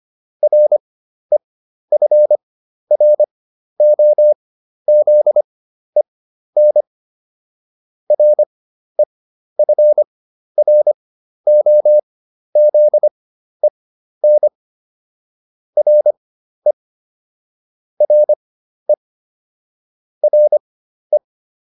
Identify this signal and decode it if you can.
Morse code, heard as a keyed tone, about 25 wpm, spaced as for 12 wpm, 600 Hz. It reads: REFROZEN REFROZEN RE RE RE